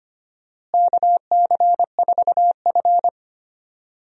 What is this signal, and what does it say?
Morse code, heard as a keyed tone, about 25 wpm, 700 Hz. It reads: KC4F
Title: USA Call Signs - 4 Characters - Repeated 2x - 25wpm